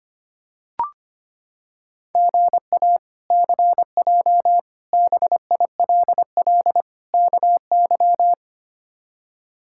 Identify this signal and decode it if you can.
Morse code, heard as a keyed tone, about 25 wpm, 700 Hz. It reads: GA CJ BILL KY
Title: K1USN SST QSO Practice - 25wpm